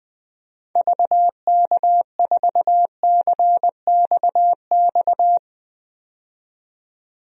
Morse code, heard as a keyed tone, about 20 wpm, 700 Hz. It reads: VK4CXX
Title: DXCC-100 Call Signs - 6 Characters - 20wpm